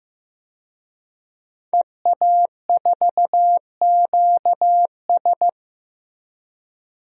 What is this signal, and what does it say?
Morse code, heard as a keyed tone, about 15 wpm, 700 Hz. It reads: EA4QS
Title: DXCC-100 Call Signs - 5 Characters - 15wpm